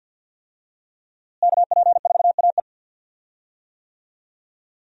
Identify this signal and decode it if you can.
Morse code, heard as a keyed tone, about 52 wpm, 700 Hz. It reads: KP4AE
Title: USA Call Signs - 5 Characters - Repeated 2x - 50wpm